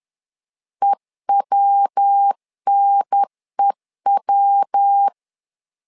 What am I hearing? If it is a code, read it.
Morse code, heard as a keyed tone, about 10 wpm, 800 Hz. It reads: EWNEW
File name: LEZIONI / 3 MWRO / 3 MW 10wpm 1959